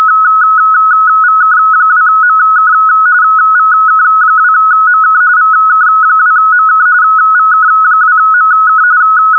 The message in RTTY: VK2MK VK2MK VK2MK VK2MK